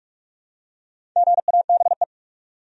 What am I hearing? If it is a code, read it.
Morse code, heard as a keyed tone, about 46 wpm, 700 Hz. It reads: GABE
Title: USA Common Names - Max 4 Letters - 45wpm